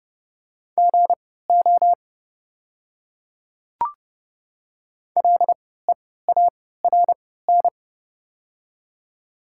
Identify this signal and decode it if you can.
Morse code, heard as a keyed tone, about 30 wpm, spaced as for 17 wpm, 700 Hz. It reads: GO LEARN
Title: 3x Character Spacing - Top 300 Words - 30wpm